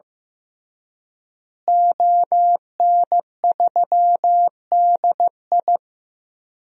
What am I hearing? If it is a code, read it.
Morse code, heard as a keyed tone, about 15 wpm, 700 Hz. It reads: ON3DI
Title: DXCC-100 Call Signs - 5 Characters - Repeated 3x - 15wpm